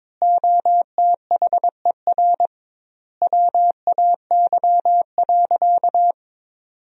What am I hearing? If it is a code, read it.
Morse code, heard as a keyed tone, about 22 wpm, 700 Hz. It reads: OTHER WAY.
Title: 2x Word-Spacing - Sentences from Top 100 Words - 22wpm